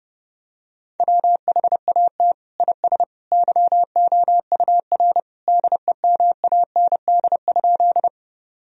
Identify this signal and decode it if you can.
Morse code, heard as a keyed tone, about 30 wpm, 700 Hz. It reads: WHAT IS YOUR DEMAND?
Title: Sentences from Top 1000 Words - Review Part 4 - 30wpm